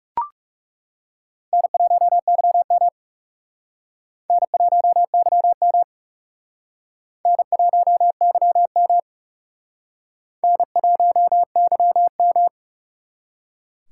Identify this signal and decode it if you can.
Morse code, heard as a keyed tone, about 36 wpm, 700 Hz. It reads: N1YM N1YM N1YM N1YM
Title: Speed Racing - USA 1x2 Call Signs 30wpm